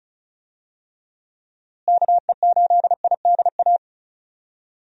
Morse code, heard as a keyed tone, about 36 wpm, 700 Hz. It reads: KE8IDA
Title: USA Call Signs - 6 Characters - Repeated 3x - 35wpm